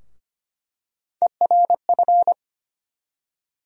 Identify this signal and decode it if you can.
Morse code, heard as a keyed tone, about 25 wpm, 700 Hz. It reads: ERF
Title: USA Call Sign Postfixes - 3 Letters 25wpm